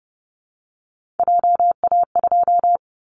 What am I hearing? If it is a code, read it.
Morse code, heard as a keyed tone, about 30 wpm, 700 Hz. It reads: JA2
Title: Callsign Prefixes for Top 25 DXCC - Repeated 3x - 30wpm